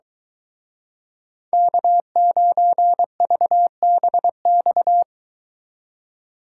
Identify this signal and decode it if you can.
Morse code, heard as a keyed tone, about 23 wpm, 700 Hz. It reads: K9VBX